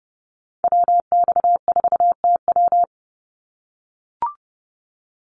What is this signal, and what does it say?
Morse code, heard as a keyed tone, about 30 wpm, 700 Hz. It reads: WX4TW
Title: NA SOTA Activator Callsigns 30wpm